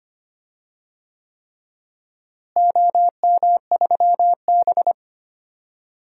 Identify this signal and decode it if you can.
Morse code, heard as a keyed tone, about 25 wpm, 700 Hz. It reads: OM3B
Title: DXCC-100 Call Signs - 4 Characters - Repeated 3x - 25wpm